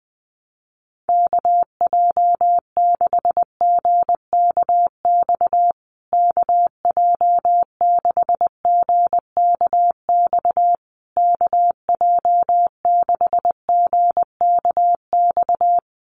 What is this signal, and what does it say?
Morse code, heard as a keyed tone, about 20 wpm, 700 Hz. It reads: KJ6GKX KJ6GKX KJ6GKX
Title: USA Call Signs Repeated 3x 20wpm